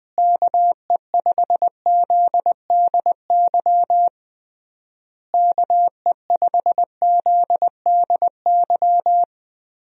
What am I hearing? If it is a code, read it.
Morse code, heard as a keyed tone, about 20 wpm, 700 Hz. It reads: KE5ZDY KE5ZDY
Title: USA Call Signs - 6 Characters - Repeated 3x - 20wpm